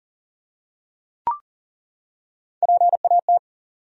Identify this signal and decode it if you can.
Morse code, heard as a keyed tone, about 40 wpm, 700 Hz. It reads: PAT
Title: USA Common Names - Max 5 Letters - 40wpm